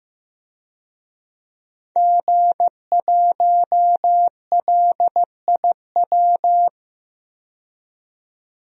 Morse code, heard as a keyed tone, about 15 wpm, 700 Hz. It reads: G1LIW
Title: DXCC-100 Call Signs - 5 Characters - 15wpm